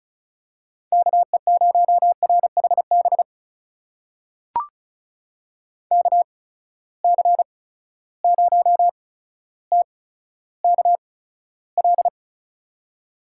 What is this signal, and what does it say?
Morse code, heard as a keyed tone, about 36 wpm, spaced as for 15 wpm, 700 Hz. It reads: KE0RHB KC0TKL
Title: USA Call Signs - 6 Characters - 8x Character Spacing - 35wpm